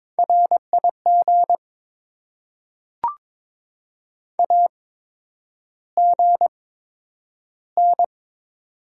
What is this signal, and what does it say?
Morse code, heard as a keyed tone, about 22 wpm, spaced as for 9 wpm, 700 Hz. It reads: RIG AGN
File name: Top 20 QSO Elements - 8x Character Spacing - 22wpm